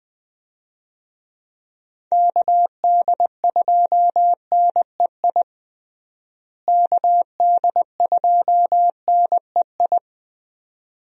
Morse code, heard as a keyed tone, about 20 wpm, 700 Hz. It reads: KD2NEI KD2NEI
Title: USA Call Signs - 6 Characters - Repeated 2x - 20wpm